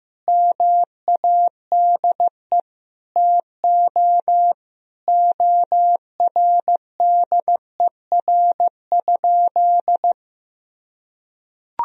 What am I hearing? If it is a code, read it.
Morse code, heard as a keyed tone, about 15 wpm, 700 Hz. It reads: MADE TO ORDER?
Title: Sentences from Top 400 Words and Abbreviations 15wpm